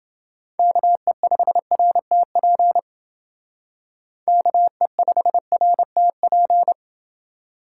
Morse code, heard as a keyed tone, about 28 wpm, 700 Hz. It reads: KE5RTP KE5RTP